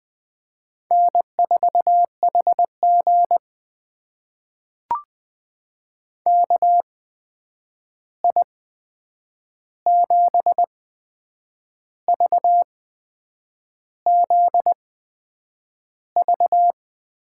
Morse code, heard as a keyed tone, about 20 wpm, spaced as for 8 wpm, 700 Hz. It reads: N4HG KI7VZV
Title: USA Call Signs - 8x Character Spacing - 20wpm